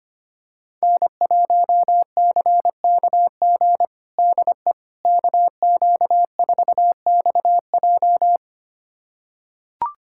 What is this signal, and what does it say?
Morse code, heard as a keyed tone, about 25 wpm, 700 Hz. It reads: N1CKG DE KQ4XJ
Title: USA Callsign DE Callsign - 5 Characters - 25wpm